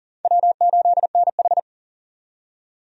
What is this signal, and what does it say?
Morse code, heard as a keyed tone, about 40 wpm, 700 Hz. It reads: W8NH
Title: USA Call Signs - 4 Characters - 40wpm